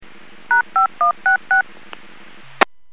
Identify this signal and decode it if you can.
DTMF, heard as telephone touch tones, tones of about 130 ms. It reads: #5166